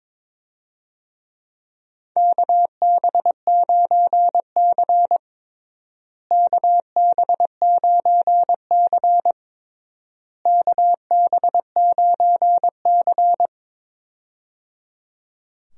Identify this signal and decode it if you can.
Morse code, heard as a keyed tone, about 22 wpm, 700 Hz. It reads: KB9C KB9C KB9C
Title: USA Call Signs - 4 Characters - Repeated 3x - 22wpm